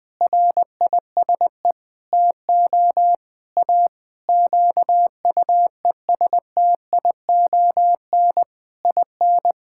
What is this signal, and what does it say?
Morse code, heard as a keyed tone, about 21 wpm, 700 Hz. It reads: RISE TO A QUESTION IN
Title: Sentences from Top 1000 Words - Review Part 6 - 20wpm